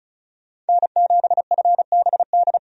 Morse code, heard as a keyed tone, about 36 wpm, 700 Hz. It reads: N7FBD
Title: USA Call Signs - 5 Characters - 35wpm